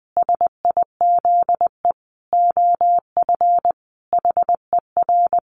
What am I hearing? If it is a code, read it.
Morse code, heard as a keyed tone, about 20 wpm, 700 Hz. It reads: SIZE OF HER
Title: Sentences from Top 1200 Words 20wpm